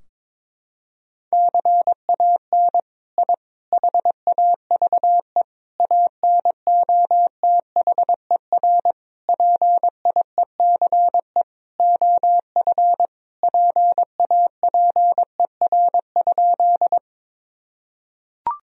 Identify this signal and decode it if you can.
Morse code, heard as a keyed tone, about 22 wpm, 700 Hz. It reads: CAN I HAVE ANOTHER PIECE OF PAPER?
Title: Sentences from Top 1000 Words - Review Part 2 - 22wpm